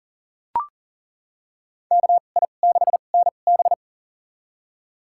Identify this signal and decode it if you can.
Morse code, heard as a keyed tone, about 38 wpm, spaced as for 28 wpm, 700 Hz. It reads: KI6NB
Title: USA Call Signs - 5 Characters - 2x Character Spacing - 40wpm